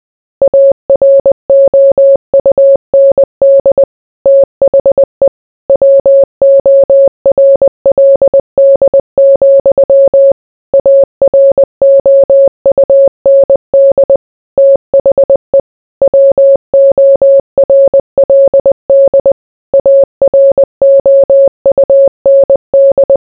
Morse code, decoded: AROUND THE WORLD, AROUND THE WORLD AROUND

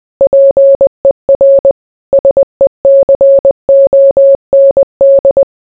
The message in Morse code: PER SECOND

